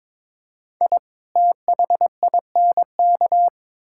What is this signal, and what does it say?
Morse code, heard as a keyed tone, about 22 wpm, 700 Hz. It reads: I THINK
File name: practice / 22wpm / Build A Phrase - 5-Word Phrases - 22wpm